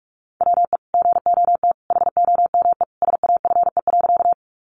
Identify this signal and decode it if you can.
Morse code, heard as a keyed tone, about 44 wpm, 700 Hz. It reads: WE GOT HOME SAFE.
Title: Sentences from Top 900 Words 45wpm